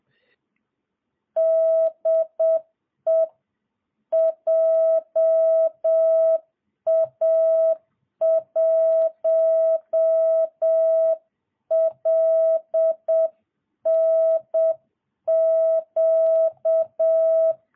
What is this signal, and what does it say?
Morse code, heard as a keyed tone, about 7 wpm, 650 Hz. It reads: DE JA1LNQ